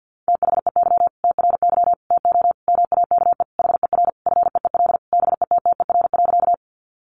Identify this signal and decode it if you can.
Morse code, heard as a keyed tone, about 50 wpm, 700 Hz. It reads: THEY TRY TO MAKE HER FEEL BETTER.
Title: Easy Phrases 50wpm